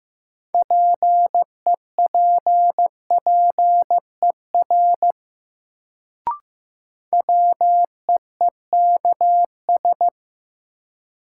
Morse code, heard as a keyed tone, about 15 wpm, 700 Hz. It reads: PEPPER WEEKS
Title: Sets of 5 Words - New Word Drill - 15wpm